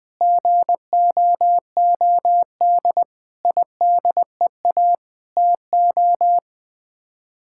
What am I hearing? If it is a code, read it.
Morse code, heard as a keyed tone, about 21 wpm, 700 Hz. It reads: GOOD IDEA TO
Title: Sets of 8 Words - 20wpm